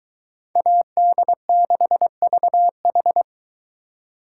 Morse code, heard as a keyed tone, about 22 wpm, 700 Hz. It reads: AD6VH